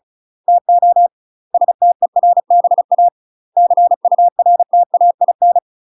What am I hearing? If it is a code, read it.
Morse code, heard as a keyed tone, about 36 wpm, 700 Hz. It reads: TO STERBA CURTAIN
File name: UK MASTER QSOs TRAINING / 35WPM / UK MASTER QSOs 0017